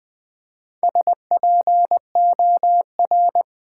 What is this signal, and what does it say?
Morse code, heard as a keyed tone, about 20 wpm, 700 Hz. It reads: SPOR
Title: Build A Word - 1 Letter At A Time - 5-Letter Words - 20wpm